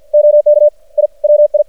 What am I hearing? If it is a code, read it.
Morse code, heard as a keyed tone, about 50 wpm, 600 Hz. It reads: OK I WI
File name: archived / CRSnet